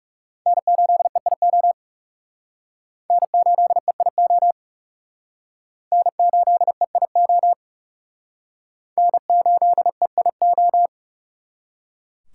Morse code, heard as a keyed tone, about 36 wpm, 700 Hz. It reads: N8EIO N8EIO N8EIO N8EIO
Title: Speed Racing - USA 1x3 Call Signs 30wpm